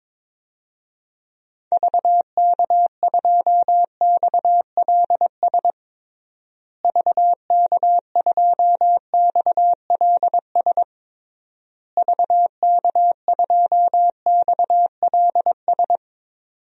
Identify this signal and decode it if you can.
Morse code, heard as a keyed tone, about 22 wpm, 700 Hz. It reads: VK2XLS VK2XLS VK2XLS
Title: DXCC-100 Call Signs - 6 Characters - Repeated 3x - 22wpm